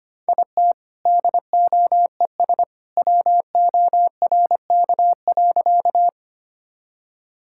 Morse code, heard as a keyed tone, about 25 wpm, 700 Hz. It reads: IT DOES WORK.
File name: practice / 25wpm / Sentences with 2-Word Phrases - 25wpm